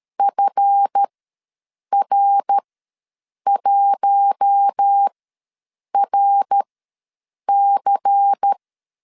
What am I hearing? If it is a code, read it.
Morse code, heard as a keyed tone, about 13 wpm, spaced as for 7 wpm, 800 Hz. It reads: FR1RC
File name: LEZIONI / 7 GF9Z / 7 GF ws 12wpm 3140